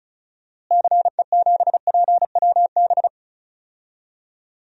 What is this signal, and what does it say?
Morse code, heard as a keyed tone, about 36 wpm, 700 Hz. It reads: CE7PWB
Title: DXCC-100 Call Signs - 6 Characters - Repeated 2x - 35wpm